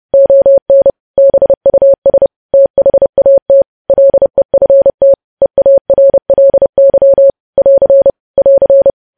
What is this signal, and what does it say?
Morse code, heard as a keyed tone, about 30 wpm, 550 Hz. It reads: ON BUS THAT LEFT EARLY <AR> <AR>
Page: news headlines in morse code at 30 wpm